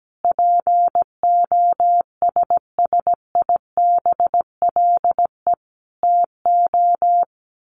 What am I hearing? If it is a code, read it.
Morse code, heard as a keyed tone, about 17 wpm, 700 Hz. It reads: POSSIBLE TO
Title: Sets of 5 Words Encore2 17wpm